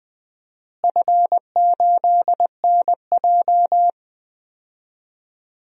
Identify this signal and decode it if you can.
Morse code, heard as a keyed tone, about 20 wpm, 700 Hz. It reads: F8NJ
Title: DXCC-100 Call Signs - 4 Characters - Repeated 3x - 20wpm